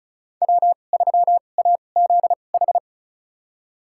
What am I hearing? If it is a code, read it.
Morse code, heard as a keyed tone, about 36 wpm, spaced as for 25 wpm, 700 Hz. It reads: W3AZH
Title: USA Call Signs - 2x Character Spacing - 35wpm